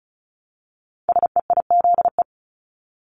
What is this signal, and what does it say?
Morse code, heard as a keyed tone, about 35 wpm, 700 Hz. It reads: SEIZE
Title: Multi-letter Phonemes Repeated 3x 35wpm